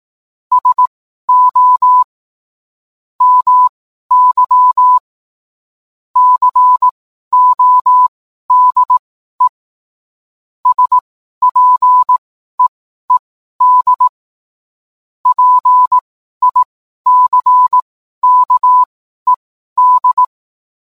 Morse code, heard as a keyed tone, about 18 wpm, spaced as for 12 wpm, 1000 Hz. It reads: SO MY CODE SPEED PICKED
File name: cwblog / cw mp3 directory / mem morse 12